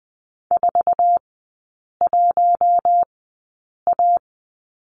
Morse code, heard as a keyed tone, about 20 wpm, spaced as for 8 wpm, 690 Hz. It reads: 41A
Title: 081 Lesson 14e - Number 2 - Three Characters 20wpm